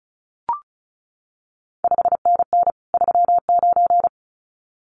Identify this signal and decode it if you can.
Morse code, heard as a keyed tone, about 36 wpm, 700 Hz. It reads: 5NN 39
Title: CQ WW DX Contest Exchange Encore - 35wpm